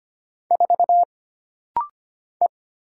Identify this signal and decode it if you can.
Morse code, heard as a keyed tone, about 25 wpm, 700 Hz. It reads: 4 E
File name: Single Character (Letter:Number:Punctuation:Prosign) - Rapid-Fire - 25wpm